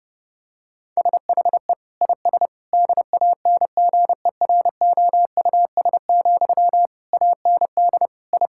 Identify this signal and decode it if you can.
Morse code, heard as a keyed tone, about 32 wpm, 700 Hz. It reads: SHE IS DANGEROUS, AND I
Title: Sentences from Top 1600 Words 30wpm